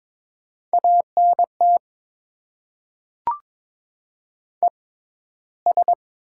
Morse code, heard as a keyed tone, about 22 wpm, spaced as for 13 wpm, 700 Hz. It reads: ANT ES
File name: Top 20 QSO Elements - 6x Character Spacing - 22wpm